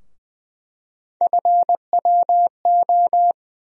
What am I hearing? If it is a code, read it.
Morse code, heard as a keyed tone, about 20 wpm, 700 Hz. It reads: FWO
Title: USA Call Sign Postfixes - 3 Letters 20wpm